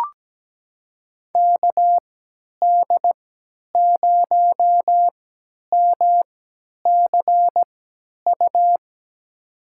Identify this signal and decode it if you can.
Morse code, heard as a keyed tone, about 17 wpm, spaced as for 10 wpm, 700 Hz. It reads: KD0MCU
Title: USA Call Signs - 6 Characters - 3x Character Spacing - 17wpm